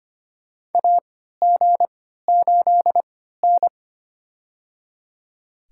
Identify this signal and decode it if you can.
Morse code, heard as a keyed tone, about 25 wpm, spaced as for 14 wpm, 700 Hz. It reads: AG8N
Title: USA Call Signs - 4 Characters - 3x Character Spacing - 25wpm